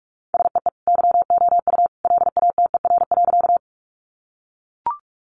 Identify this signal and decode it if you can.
Morse code, heard as a keyed tone, about 44 wpm, 700 Hz. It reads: SEE YOU LATER.
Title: Sentences from Top 700 Words 45wpm